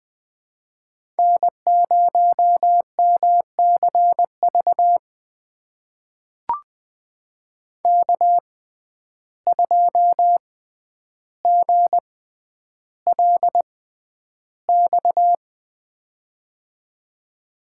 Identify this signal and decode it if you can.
Morse code, heard as a keyed tone, about 20 wpm, spaced as for 10 wpm, 700 Hz. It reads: N0MCV K2GLX
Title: USA Call Signs - 5 Characters - 6x Character Spacing - 20wpm